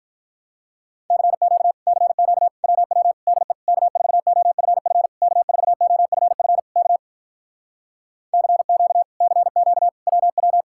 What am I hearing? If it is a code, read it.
Morse code, heard as a keyed tone, about 48 wpm, 700 Hz. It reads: CQ CQ WW DE K4OFF K4OFF K CQ CQ WW